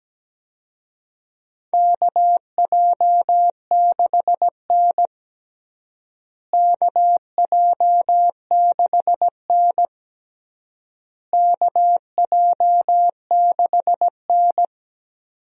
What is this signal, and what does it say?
Morse code, heard as a keyed tone, about 17 wpm, 700 Hz. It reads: KJ6N KJ6N KJ6N
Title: USA Call Signs - 4 Characters - Repeated 3x - 17wpm